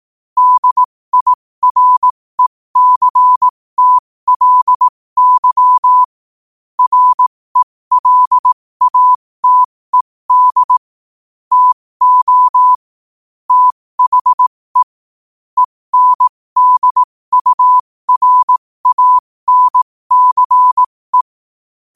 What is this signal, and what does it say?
Morse code, heard as a keyed tone, about 18 wpm, spaced as for 15 wpm, 1000 Hz. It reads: DIRECTLY RELATED TO THE ENDURANCE
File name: qotdfiles / 15wpm